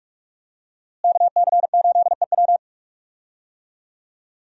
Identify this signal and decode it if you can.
Morse code, heard as a keyed tone, about 42 wpm, 700 Hz. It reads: KC8EW
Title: USA Call Signs - 5 Characters - Repeated 3x - 45wpm